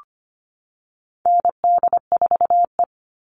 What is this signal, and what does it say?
Morse code, heard as a keyed tone, about 25 wpm, 700 Hz. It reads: ND4E
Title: All Cut Numbers - 4-Digit 25wpm